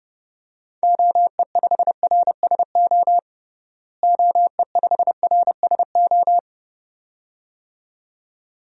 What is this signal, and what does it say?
Morse code, heard as a keyed tone, about 32 wpm, 700 Hz. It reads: OE5RSO OE5RSO
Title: DXCC-100 Call Signs - 6 Characters - Repeated 2x - 30wpm